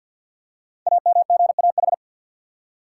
Morse code, heard as a keyed tone, about 48 wpm, 690 Hz. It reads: AMGAH
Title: Code Groups - 5 Letters - 50wpm